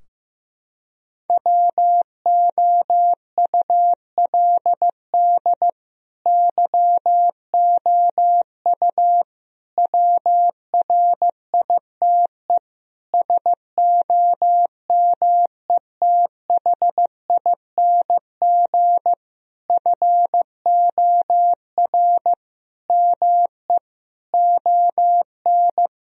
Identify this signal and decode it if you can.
Morse code, heard as a keyed tone, about 16 wpm, 700 Hz. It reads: WOULD YOU WRITE SOMETHING FOR ME ON
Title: Sentences from Top 1000 Words - Review Part 2 - 15wpm